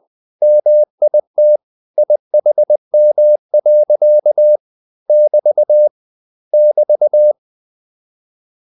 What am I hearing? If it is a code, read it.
Morse code, heard as a keyed tone, about 20 wpm, 600 Hz. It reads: MIT IHM. <BT> <BT>